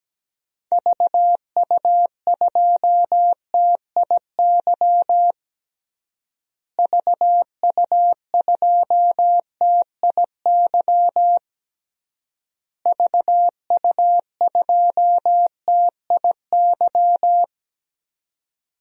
Morse code, heard as a keyed tone, about 17 wpm, 700 Hz. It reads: VU2TIY VU2TIY VU2TIY
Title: DXCC-100 Call Signs - 6 Characters - Repeated 3x - 17wpm